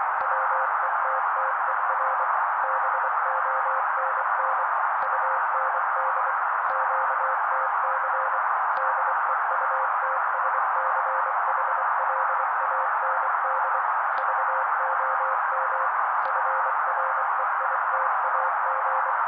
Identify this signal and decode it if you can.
Morse code, heard as a keyed tone, about 24 wpm, 530 Hz. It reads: WETTER BONN UND QTC DEUTSCHLAND VOM FREITAG
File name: daily / 23wpm noise